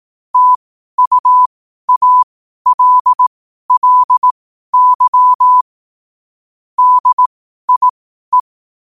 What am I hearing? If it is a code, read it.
Morse code, heard as a keyed tone, about 18 wpm, spaced as for 12 wpm, 1000 Hz. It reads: TUALLY DIE
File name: qotdfiles / 12wpm